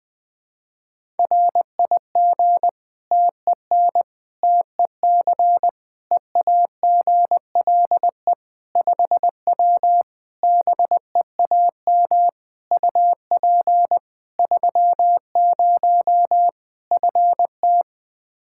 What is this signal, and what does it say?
Morse code, heard as a keyed tone, about 21 wpm, 700 Hz. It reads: RIG TEN TEC EAGLE 5W BEAM UP 30 FT
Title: Practice QSO - Exchange 2 - Info - 20wpm